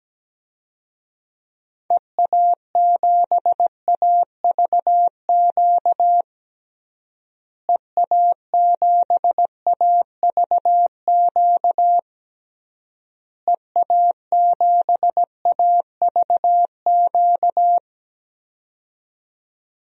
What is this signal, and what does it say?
Morse code, heard as a keyed tone, about 17 wpm, 700 Hz. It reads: EA7AVQ EA7AVQ EA7AVQ
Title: DXCC-100 Call Signs - 6 Characters - Repeated 3x - 17wpm